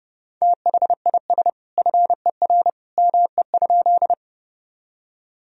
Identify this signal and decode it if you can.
Morse code, heard as a keyed tone, about 30 wpm, 700 Hz. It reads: THIS FER ME?
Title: Sentences from Top 200 Words and Abbreviations 30wpm